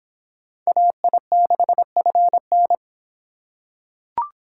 Morse code, heard as a keyed tone, about 25 wpm, 700 Hz. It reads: AI6FN